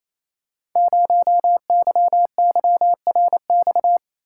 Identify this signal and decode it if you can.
Morse code, heard as a keyed tone, about 29 wpm, 700 Hz. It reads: 0YYRX
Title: Code Groups - 5 Letter:Numbers - 28wpm